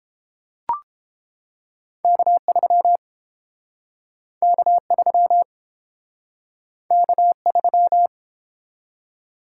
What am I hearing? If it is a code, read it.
Morse code, heard as a keyed tone, about 32 wpm, 700 Hz. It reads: K3 K3 K3